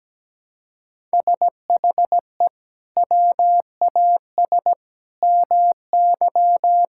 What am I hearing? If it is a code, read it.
Morse code, heard as a keyed tone, about 17 wpm, 700 Hz. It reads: SHE WAS MY
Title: Sentences from Top 1000 Words - Review Part 6 - 17wpm